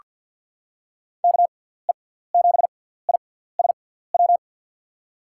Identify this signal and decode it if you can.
Morse code, heard as a keyed tone, about 47 wpm, spaced as for 17 wpm, 700 Hz. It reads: KE7ISW
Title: USA Call Signs - 6 Characters - 6x Character Spacing - 50wpm